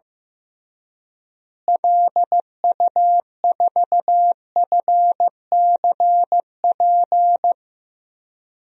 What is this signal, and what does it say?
Morse code, heard as a keyed tone, about 15 wpm, 700 Hz. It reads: LU4FCP
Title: DXCC-100 Call Signs - 6 Characters - Repeated 2x - 15wpm